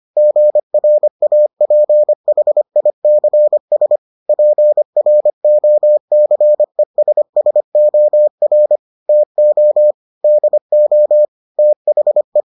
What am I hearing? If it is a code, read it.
Morse code, heard as a keyed tone, about 25 wpm, 600 Hz. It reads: GRAPHICS PROCESSOR TO DO THE